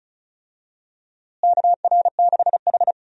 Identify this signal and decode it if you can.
Morse code, heard as a keyed tone, about 36 wpm, 700 Hz. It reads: KR6H
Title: USA Call Signs - 4 Characters - Repeated 2x - 35wpm